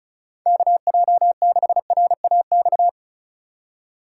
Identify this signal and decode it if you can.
Morse code, heard as a keyed tone, about 36 wpm, 700 Hz. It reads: KJ6RAX
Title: USA Call Signs - 4x Character Spacing - 35wpm